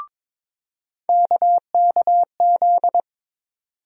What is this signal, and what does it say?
Morse code, heard as a keyed tone, about 22 wpm, 700 Hz. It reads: KKZ
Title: USA Call Sign Postfixes - 3 Letters 22wpm